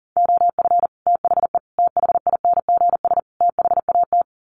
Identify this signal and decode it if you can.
Morse code, heard as a keyed tone, about 40 wpm, 700 Hz. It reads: OF THE THINGS THAT
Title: Sets of 4 Words 40wpm